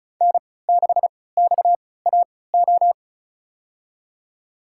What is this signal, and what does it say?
Morse code, heard as a keyed tone, about 34 wpm, spaced as for 20 wpm, 700 Hz. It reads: N6XAO
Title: USA Call Signs - 3x Character Spacing - 35wpm